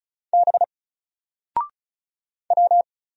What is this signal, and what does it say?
Morse code, heard as a keyed tone, about 36 wpm, 700 Hz. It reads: B W
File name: Single-Letter-Number-35wpm